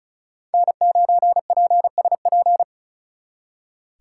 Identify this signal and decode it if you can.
Morse code, heard as a keyed tone, about 43 wpm, 700 Hz. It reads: N9PSP
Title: Speed Racing - USA 1x3 Call Signs 35wpm